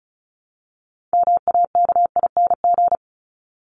Morse code, heard as a keyed tone, about 34 wpm, 700 Hz. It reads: MAKING
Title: ING Suffix - Top 100 Words with Base Words 35wpm